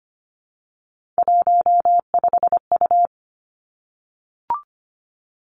Morse code, heard as a keyed tone, about 25 wpm, 700 Hz. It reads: 15U
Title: Three Challenging Letter-Number Combinations 25wpm